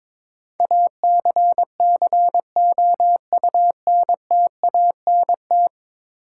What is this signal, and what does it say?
Morse code, heard as a keyed tone, about 22 wpm, 700 Hz. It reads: ACCOUNTANT
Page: morse-code-ninja-practice-22wpm